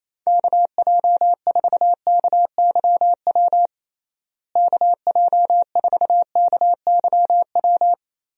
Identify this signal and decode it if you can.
Morse code, heard as a keyed tone, about 28 wpm, 700 Hz. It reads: KJ4KYW KJ4KYW
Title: USA Call Signs - 6 Characters - Repeated 3x - 28wpm